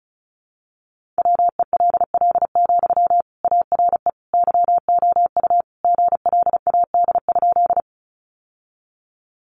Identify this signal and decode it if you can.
Morse code, heard as a keyed tone, about 34 wpm, 700 Hz. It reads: WELL, ARE YOU GLAD?
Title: Sentences from Top 800 Words 35wpm